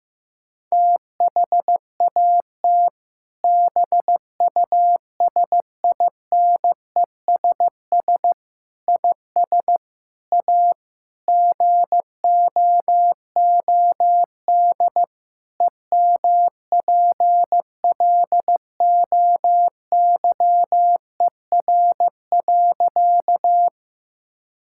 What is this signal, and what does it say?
Morse code, heard as a keyed tone, about 16 wpm, 700 Hz. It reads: THAT BUSINESS IS A GOOD EMPLOYER.
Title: Sentences from Top 1600 Words 15wpm